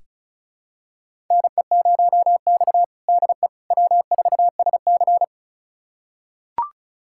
Morse code, heard as a keyed tone, about 36 wpm, 700 Hz. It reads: NE0X DE W4SC
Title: USA Callsign DE Callsign - 4 Characters - 35wpm